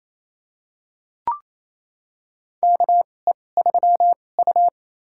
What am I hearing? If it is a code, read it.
Morse code, heard as a keyed tone, about 28 wpm, spaced as for 20 wpm, 700 Hz. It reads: KE3U
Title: USA Call Signs - 4 Characters - 2x Character Spacing - 28wpm